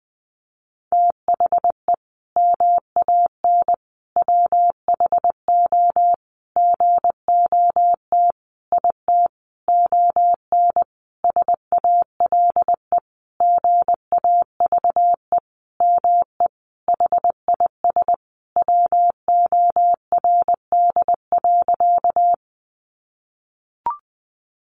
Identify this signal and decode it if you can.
Morse code, heard as a keyed tone, about 19 wpm, 700 Hz. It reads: THE MAN WHO GOT IT ON SALE GAVE ME HIS WORD.
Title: Sentences from Top 1400 Words 20wpm